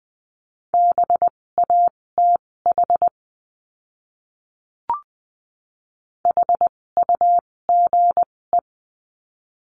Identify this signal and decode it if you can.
Morse code, heard as a keyed tone, about 20 wpm, spaced as for 16 wpm, 700 Hz. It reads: BATH HUGE
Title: 256 Lesson 42a - Words 12-20wpm